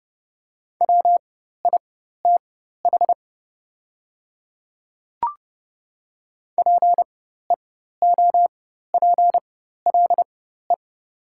Morse code, heard as a keyed tone, about 30 wpm, spaced as for 14 wpm, 700 Hz. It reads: WITH PEOPLE
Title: 4x Character Spacing - Top 100 Words - 30wpm